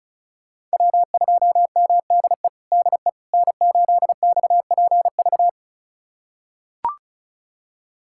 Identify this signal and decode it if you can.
Morse code, heard as a keyed tone, about 36 wpm, 700 Hz. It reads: W2MDE DE N8XPV
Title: USA Callsign DE Callsign - 5 Characters - 35wpm